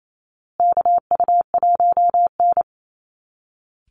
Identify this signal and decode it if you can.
Morse code, heard as a keyed tone, about 28 wpm, 700 Hz. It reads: KU1N
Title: Call Signs Repeated 2x - 28wpm